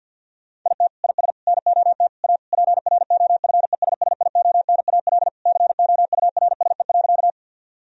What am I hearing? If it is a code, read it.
Morse code, heard as a keyed tone, about 50 wpm, 700 Hz. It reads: IT IS NOT A PROFESSIONAL COURSE.
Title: Sentences from Top 1000 Words - Review Part 4 - 50wpm